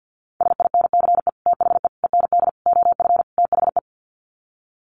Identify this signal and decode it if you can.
Morse code, heard as a keyed tone, about 50 wpm, 700 Hz. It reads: SINCE THE END OF THE
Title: Sets of 5 Words Encore 50wpm